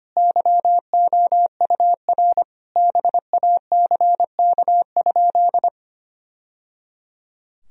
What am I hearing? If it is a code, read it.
Morse code, heard as a keyed tone, about 25 wpm, 700 Hz. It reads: YOUR BACK?
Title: Sentences from Top 1000 Words - Review Part 2 - 25wpm